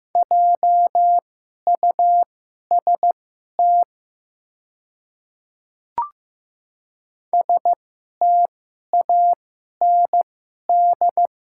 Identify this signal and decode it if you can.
Morse code, heard as a keyed tone, about 15 wpm, spaced as for 11 wpm, 700 Hz. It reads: JUST STAND
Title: 2x Character Spacing - Top 300 Words - 15wpm